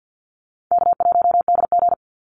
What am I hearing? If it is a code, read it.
Morse code, heard as a keyed tone, about 50 wpm, 700 Hz. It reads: K1DG
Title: Call Signs Repeated 2x - 50wpm